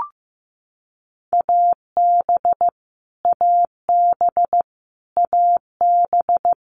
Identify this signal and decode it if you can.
Morse code, heard as a keyed tone, about 15 wpm, 700 Hz. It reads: AB AB AB
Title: Canadian Provinces and Territories - Repeated 3x - 15wpm